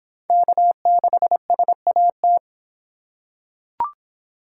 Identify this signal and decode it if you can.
Morse code, heard as a keyed tone, about 26 wpm, 700 Hz. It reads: K6SAT